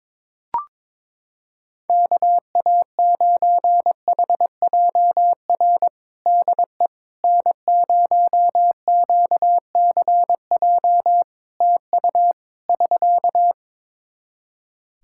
Morse code, heard as a keyed tone, about 22 wpm, 700 Hz. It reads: KA9HJR DE N0QCJ TU <SK>
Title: Practice QSO - Exchange 4 - Ending - 22wpm